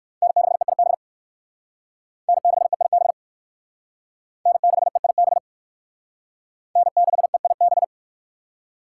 Morse code, heard as a keyed tone, about 56 wpm, 700 Hz. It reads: N6EIB N6EIB N6EIB N6EIB